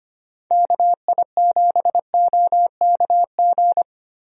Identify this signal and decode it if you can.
Morse code, heard as a keyed tone, about 25 wpm, 700 Hz. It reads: KI7OKG
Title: USA Call Signs - 6 Characters - 25wpm